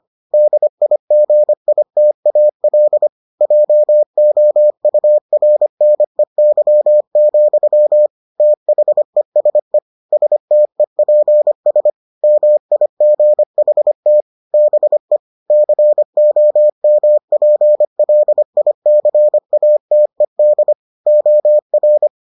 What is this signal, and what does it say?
Morse code, heard as a keyed tone, about 25 wpm, 600 Hz. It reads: DIGITAL JOURNEY, THESE STEPS MIGHT BE COMPLICATED OR